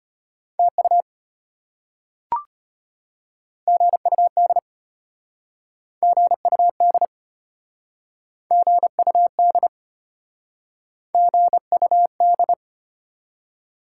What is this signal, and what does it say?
Morse code, heard as a keyed tone, about 33 wpm, 700 Hz. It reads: TU GUD GUD GUD GUD